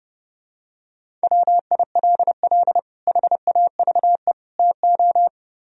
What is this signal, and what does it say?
Morse code, heard as a keyed tone, about 29 wpm, 700 Hz. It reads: WILL HAVE TO